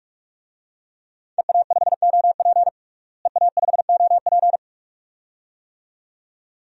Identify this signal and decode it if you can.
Morse code, heard as a keyed tone, about 46 wpm, 700 Hz. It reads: EA5OP EA5OP
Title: DXCC-100 Call Signs - 5 Characters - Repeated 2x - 45wpm